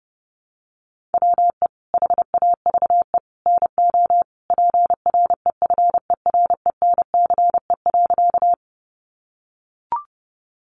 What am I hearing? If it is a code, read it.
Morse code, heard as a keyed tone, about 28 wpm, 700 Hz. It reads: WE HAVE NO PREFERENCE.
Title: Sentences from Top 1500 Words 30wpm